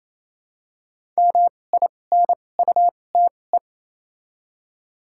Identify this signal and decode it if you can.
Morse code, heard as a keyed tone, about 28 wpm, spaced as for 20 wpm, 700 Hz. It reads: MINUTE
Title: 2x Character Spacing - Top 500 Words - 28wpm